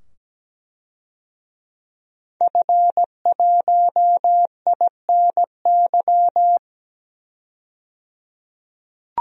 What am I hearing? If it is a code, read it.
Morse code, heard as a keyed tone, about 17 wpm, 700 Hz. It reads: F1INY
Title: DXCC-100 Call Signs - 5 Characters - 17wpm